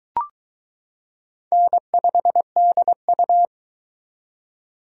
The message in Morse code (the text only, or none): N5DU